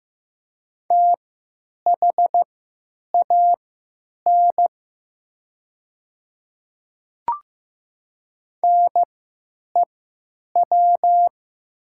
Morse code, heard as a keyed tone, about 15 wpm, spaced as for 9 wpm, 700 Hz. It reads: THAN NEW
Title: 3x Character Spacing - Top 300 Words - 15wpm